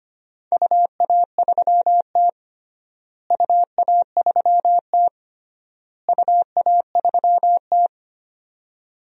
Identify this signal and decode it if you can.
Morse code, heard as a keyed tone, about 25 wpm, 700 Hz. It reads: UA3T UA3T UA3T
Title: DXCC-100 Call Signs - 4 Characters - Repeated 3x - 25wpm